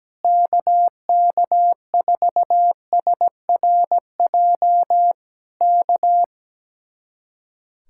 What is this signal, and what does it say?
Morse code, heard as a keyed tone, about 17 wpm, 700 Hz. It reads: KK4SRJ K
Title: Speed-Racing Common QSO Phrases - 17wpm